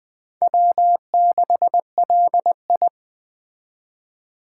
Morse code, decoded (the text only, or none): W6LI